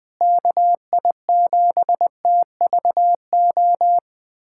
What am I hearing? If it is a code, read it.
Morse code, heard as a keyed tone, about 21 wpm, 700 Hz. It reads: KI7TVO
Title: USA Callsign DE Callsign - 6 Characters - 20wpm